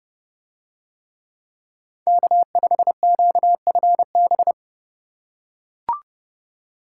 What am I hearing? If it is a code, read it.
Morse code, heard as a keyed tone, about 29 wpm, 700 Hz. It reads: K5QFB